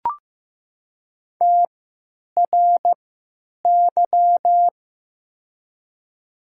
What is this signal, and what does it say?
Morse code, heard as a keyed tone, about 15 wpm, spaced as for 9 wpm, 700 Hz. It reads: TRY
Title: 3x Character Spacing - Top 500 Words - 15wpm